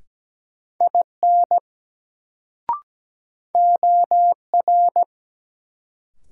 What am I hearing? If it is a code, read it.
Morse code, heard as a keyed tone, about 17 wpm, 700 Hz. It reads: IN OR
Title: Sentences from Top 100 Words - New Word Drill - 17wpm